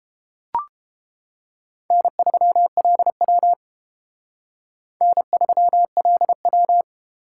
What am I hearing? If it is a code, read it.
Morse code, heard as a keyed tone, about 30 wpm, 700 Hz. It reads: N3LW N3LW